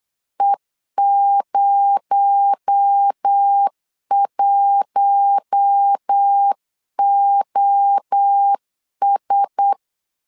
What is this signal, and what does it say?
Morse code, heard as a keyed tone, about 8 wpm, 800 Hz. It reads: E01OS
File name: CW / DET / 1 6 1500C S B 8wpm 5142